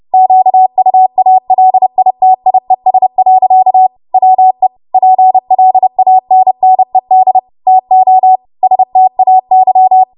Morse code, heard as a keyed tone, about 30 wpm, 750 Hz. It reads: QUALITIES. WE PLANNED TO STAY
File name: Morse / 30WPM